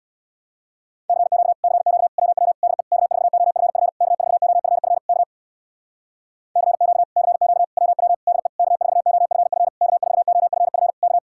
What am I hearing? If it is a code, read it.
Morse code, heard as a keyed tone, about 64 wpm, 700 Hz. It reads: CQ CQ WW DE K4OFF K4OFF K CQ CQ WW DE K4OFF K4OFF K